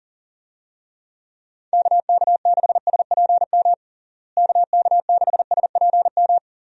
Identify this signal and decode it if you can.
Morse code, heard as a keyed tone, about 40 wpm, 700 Hz. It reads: KK6SPM KK6SPM
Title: USA Call Signs - 6 Characters - Repeated 2x - 40wpm